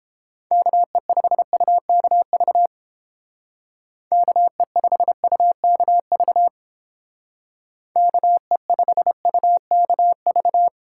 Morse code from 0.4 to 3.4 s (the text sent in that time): KE5UKV